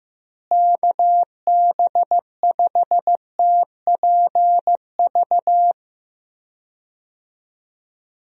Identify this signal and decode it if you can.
Morse code, heard as a keyed tone, about 15 wpm, 700 Hz. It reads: KB5TPV
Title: USA Call Signs - 6 Characters - Repeated 3x - 15wpm